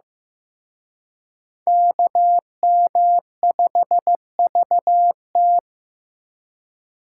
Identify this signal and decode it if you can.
Morse code, heard as a keyed tone, about 18 wpm, 700 Hz. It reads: KM5VT
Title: Speed Racing - USA 2x2 Call Signs 15wpm